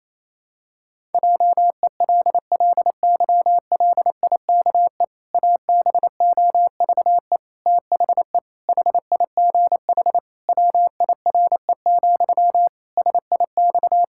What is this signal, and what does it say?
Morse code, heard as a keyed tone, about 28 wpm, 700 Hz. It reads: JELLYLIKE ABOVE THE HIGH WIRE, SIX
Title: Pangram Challenge v2 - 28wpm